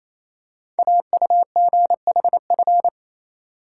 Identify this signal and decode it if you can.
Morse code, heard as a keyed tone, about 28 wpm, 700 Hz. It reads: AUGHF
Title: Code Groups - 5 Letters - 28wpm